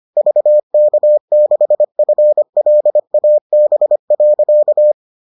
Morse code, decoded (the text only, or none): VK6FLAB.